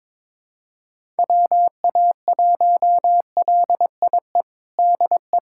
Morse code, heard as a keyed tone, about 22 wpm, 700 Hz. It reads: WA1LIE DE
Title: USA Callsign DE Callsign - 6 Characters - 22wpm